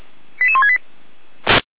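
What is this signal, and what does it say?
Five-tone selective call (ZVEI), heard as quick runs of five tones, tones of about 70 ms. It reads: 88168
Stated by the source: ZVEI five-tone selective call received on an SDR, archived by Feuerwehr Orlamünde (f85360).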